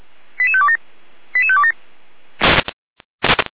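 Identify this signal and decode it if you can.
Five-tone selective call (ZVEI), heard as quick runs of five tones, tones of about 70 ms. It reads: 88527 88527
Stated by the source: ZVEI five-tone call 88505 received on SDR.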